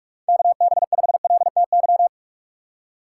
Morse code, heard as a keyed tone, about 46 wpm, 700 Hz. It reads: KB5LTY
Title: USA Call Signs - 6 Characters - 3x Character Spacing - 45wpm